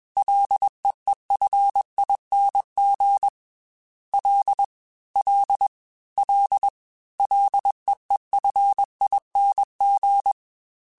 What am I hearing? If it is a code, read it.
Morse code, heard as a keyed tone, about 21 wpm, 800 Hz. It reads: LEEFING L L L LEEFING
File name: Ls / Single L m3s / Single L @20